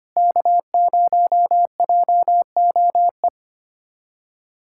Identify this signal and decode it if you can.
Morse code, heard as a keyed tone, about 25 wpm, 700 Hz. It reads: K0JOE
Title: USA Callsign DE Callsign 25wpm